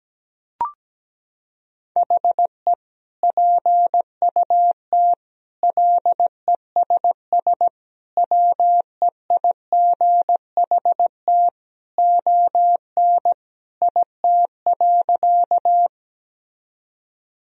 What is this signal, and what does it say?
Morse code, heard as a keyed tone, about 17 wpm, 700 Hz. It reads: HE PUT LESS WEIGHT ON IT.
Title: Sentences from Top 1000 Words - Review Part 3 - 17wpm